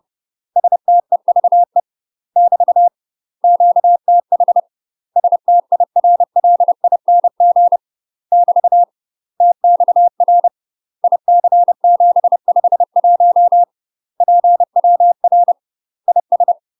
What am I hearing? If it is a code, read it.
Morse code, heard as a keyed tone, about 30 wpm, 700 Hz. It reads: STEVE <BT> QTH STIRLING <BT> TXR IC751 PWR IS
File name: UK MASTER QSOs TRAINING / 30WPM / UK MASTER QSOs 0003